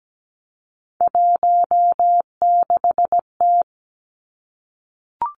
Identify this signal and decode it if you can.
Morse code, heard as a keyed tone, about 17 wpm, 700 Hz. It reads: 16T
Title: Three Challenging Letter-Number Combinations 17wpm